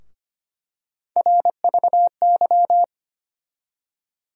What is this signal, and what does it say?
Morse code, heard as a keyed tone, about 25 wpm, 700 Hz. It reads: RVY
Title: USA Call Sign Postfixes - 3 Letters 25wpm